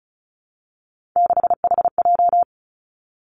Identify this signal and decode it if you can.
Morse code, heard as a keyed tone, about 36 wpm, 700 Hz. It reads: 6HJ
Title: Three Challenging Letter-Number Combinations 35wpm